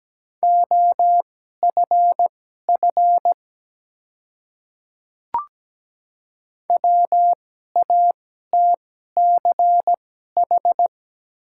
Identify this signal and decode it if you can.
Morse code, heard as a keyed tone, about 17 wpm, spaced as for 12 wpm, 700 Hz. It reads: OFF WATCH
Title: 2x Character Spacing - Top 300 Words - 17wpm